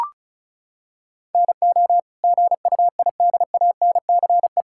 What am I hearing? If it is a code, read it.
Morse code, heard as a keyed tone, about 34 wpm, 700 Hz. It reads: NO GUIDANCE
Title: Sentences from Top 1600 Words 35wpm